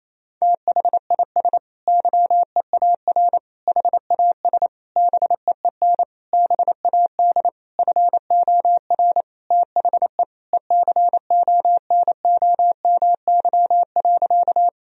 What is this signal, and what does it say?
Morse code, heard as a keyed tone, about 28 wpm, 700 Hz. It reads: THIS YEAR HAS BEEN BAD FOR THE ECONOMY.
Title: Sentences from Top 1000 Words - Review Part 3 - 28wpm